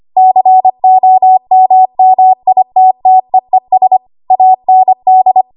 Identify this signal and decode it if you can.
Morse code, heard as a keyed tone, about 25 wpm, 750 Hz. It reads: COMMITTEES AND
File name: Morse / 25WPM